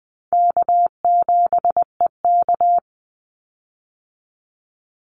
Morse code, heard as a keyed tone, about 20 wpm, 700 Hz. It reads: K7EK
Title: Easy Call Signs Repeated 2x - 20wpm